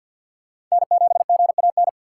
Morse code, heard as a keyed tone, about 48 wpm, 700 Hz. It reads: N7GAN
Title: USA Call Signs - 5 Characters - 4x Character Spacing - 50wpm